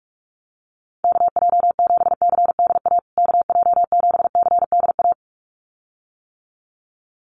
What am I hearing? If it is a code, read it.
Morse code, heard as a keyed tone, about 44 wpm, 700 Hz. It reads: KJ7CDA KJ7CDA
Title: USA Call Signs Repeated 2x 45wpm